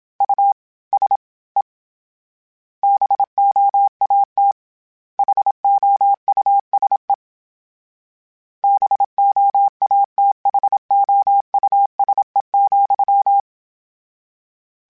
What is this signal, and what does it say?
Morse code, heard as a keyed tone, about 27 wpm, 800 Hz. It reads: U S E BOAT HOUSE BOATHOUSE,
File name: General / General-mp3s / Houses @25